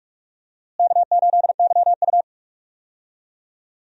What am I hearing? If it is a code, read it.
Morse code, heard as a keyed tone, about 46 wpm, 700 Hz. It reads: K8YU
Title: USA Call Signs - 4 Characters - Repeated 3x - 45wpm